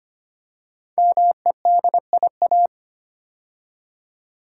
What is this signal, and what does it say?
Morse code, heard as a keyed tone, about 25 wpm, 700 Hz. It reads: MEDIA
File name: 5 Letter Words - First 3 Letters Given - 25wpm